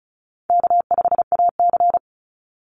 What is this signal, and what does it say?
Morse code, heard as a keyed tone, about 34 wpm, 700 Hz. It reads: K5AC
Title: USA Call Signs 35wpm